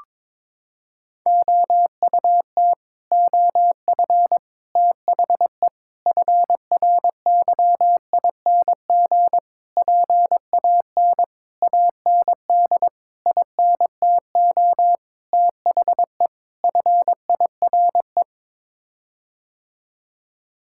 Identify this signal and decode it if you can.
Morse code, heard as a keyed tone, about 22 wpm, 700 Hz. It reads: OUT OF THE FRYING PAN AND INTO THE FIRE
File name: Sets of 9 Words - 22wpm